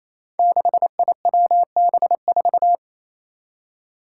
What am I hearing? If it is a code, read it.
Morse code, heard as a keyed tone, about 27 wpm, 700 Hz. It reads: 6IWB4
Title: Five Challenging Letter-Number Combinations 28wpm